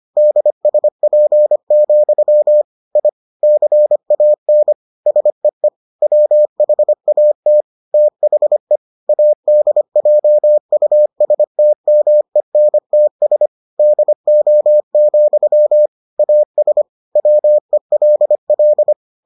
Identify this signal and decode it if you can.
Morse code, heard as a keyed tone, about 25 wpm, 600 Hz. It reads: DSP, I CAN SEE WHAT THE ADJUSTMENTS DO, AS WELL